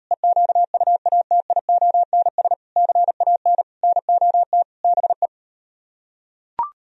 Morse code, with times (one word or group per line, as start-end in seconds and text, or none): 0.1-2.5 EQUATIONS
2.8-3.6 CAN
3.8-4.6 NOT
4.8-5.3 BE